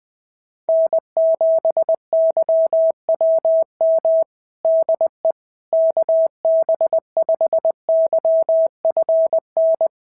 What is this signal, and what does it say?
Morse code, heard as a keyed tone, about 20 wpm, 650 Hz. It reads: N7YWM DE KB5YFN
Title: SOTA QSO Practice 20wpm